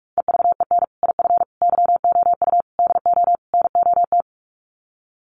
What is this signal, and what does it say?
Morse code, heard as a keyed tone, about 44 wpm, 700 Hz. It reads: EVEN IF YOU DO NOT
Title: Sets of 5 Words Encore 45wpm